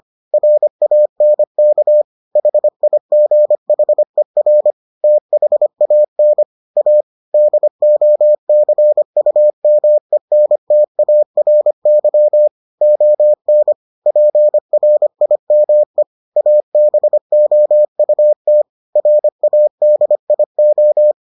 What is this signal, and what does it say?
Morse code, heard as a keyed tone, about 25 wpm, 600 Hz. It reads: RANK HIGHER THAN A DOCUMENTARY ON PRIME ABOUT RADIO